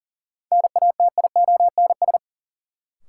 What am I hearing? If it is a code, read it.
Morse code, heard as a keyed tone, about 37 wpm, 700 Hz. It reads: NATIONS
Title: Sets of 3 Words - New Word Drill - 40wpm